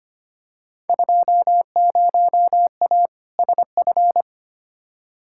Morse code, heard as a keyed tone, about 25 wpm, 700 Hz. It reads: 20A SF
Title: Field Day Exchanges Encore - 25wpm